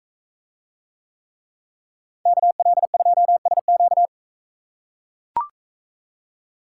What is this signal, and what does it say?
Morse code, heard as a keyed tone, about 40 wpm, 700 Hz. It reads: KL2SQ